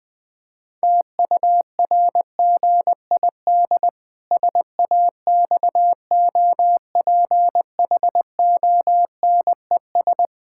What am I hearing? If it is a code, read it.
Morse code, heard as a keyed tone, about 20 wpm, 700 Hz. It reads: TURGID SAXOPHONES
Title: Pangram Challenge v2 - 20wpm